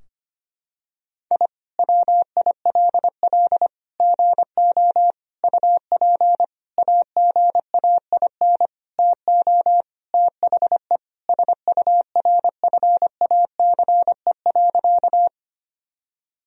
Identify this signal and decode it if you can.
Morse code, heard as a keyed tone, about 25 wpm, 700 Hz. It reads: I WILL GO UP AGAIN TO THE SURFACE.
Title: Sentences from Top 1000 Words - Review Part 3 - 25wpm